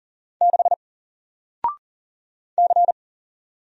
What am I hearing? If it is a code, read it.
Morse code, heard as a keyed tone, about 39 wpm, 700 Hz. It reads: 6 C
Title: Single Character (Letter:Number:Punctuation:Prosign) Alt 40wpm